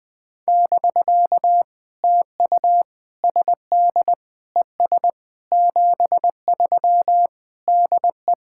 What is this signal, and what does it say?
Morse code, heard as a keyed tone, about 20 wpm, 700 Hz. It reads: <BK> TU SD ES 73 DE
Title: POTA QSO Practice - 20wpm